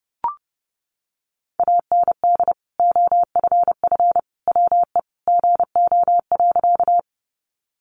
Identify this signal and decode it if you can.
Morse code, heard as a keyed tone, about 30 wpm, 700 Hz. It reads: AND OFF WE GO.
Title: Sentences from Top 200 Words 30wpm